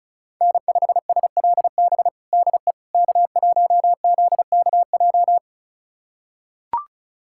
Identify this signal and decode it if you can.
Morse code, heard as a keyed tone, about 34 wpm, 700 Hz. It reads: N5SLB DE K1ZKJ
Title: USA Callsign DE Callsign 35wpm